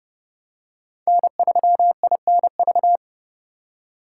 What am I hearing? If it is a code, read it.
Morse code, heard as a keyed tone, about 29 wpm, 700 Hz. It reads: N3INV